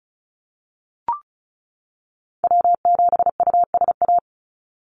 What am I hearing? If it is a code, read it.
Morse code, heard as a keyed tone, about 34 wpm, 700 Hz. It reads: W7USA
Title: NA SOTA Activator Callsigns 35wpm